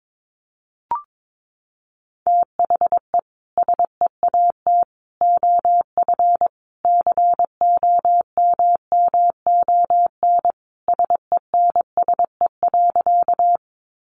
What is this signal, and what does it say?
Morse code, heard as a keyed tone, about 22 wpm, 700 Hz. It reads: THE SEAT OF COMMON SENSE.
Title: Sentences from Top 1500 Words 22wpm